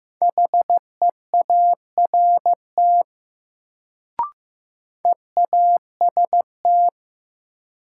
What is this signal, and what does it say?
Morse code, heard as a keyed tone, about 15 wpm, 700 Hz. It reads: HEART EAST
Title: Sets of 3 Words - New Word Drill - 15wpm